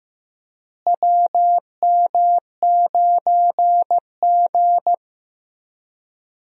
Morse code, heard as a keyed tone, about 15 wpm, 700 Hz. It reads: WM9G
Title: USA Call Signs - 4 Characters - Repeated 3x - 15wpm